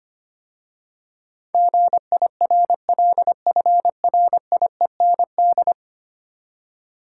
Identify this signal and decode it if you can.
Morse code, heard as a keyed tone, about 25 wpm, 700 Hz. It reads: GIRLFRIEND
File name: 10 Letter Words - First 3 Letters Given - 25wpm